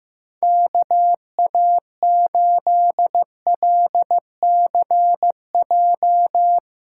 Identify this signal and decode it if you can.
Morse code, heard as a keyed tone, about 15 wpm, 700 Hz. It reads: KA8LCJ
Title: USA Callsign DE Callsign - 6 Characters - 15wpm